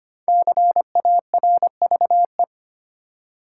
Morse code, heard as a keyed tone, about 25 wpm, 700 Hz. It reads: CARVE
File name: 5 Letter Words - First 3 Letters Given - 25wpm